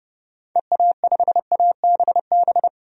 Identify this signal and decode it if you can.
Morse code, heard as a keyed tone, about 30 wpm, 700 Hz. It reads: EA5ABB
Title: DXCC-100 Call Signs - 6 Characters - 30wpm